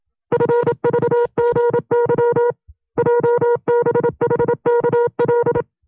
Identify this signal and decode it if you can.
Morse code, heard as a keyed tone, about 27 wpm, 480 Hz. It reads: FVGY JBHKL
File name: FreeFiles / CW-practice / 25WPM-side1